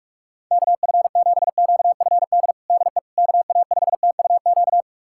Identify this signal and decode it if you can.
Morse code, heard as a keyed tone, about 46 wpm, 700 Hz. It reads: KF7QFD DE KA5TUQ
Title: USA Callsign DE Callsign - 6 Characters - 45wpm